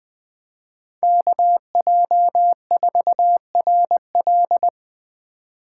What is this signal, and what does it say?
Morse code, heard as a keyed tone, about 20 wpm, 700 Hz. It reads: KJ4RL
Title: Speed Racing - USA 2x2 Call Signs 17wpm